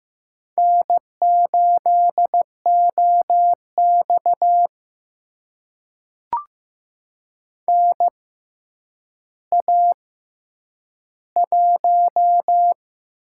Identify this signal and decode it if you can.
Morse code, heard as a keyed tone, about 15 wpm, spaced as for 9 wpm, 700 Hz. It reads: N8OX NA1
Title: USA Call Signs - 4 Characters - 6x Character Spacing - 15wpm